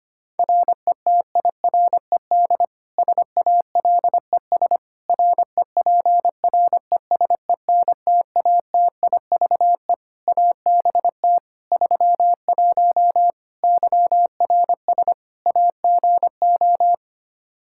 Morse code, heard as a keyed tone, about 25 wpm, 700 Hz. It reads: RETIRED SALES REPRESENTATIVE ABT 31 YRS AGO
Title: Practice QSO - Exchange 3 - More Info - 25wpm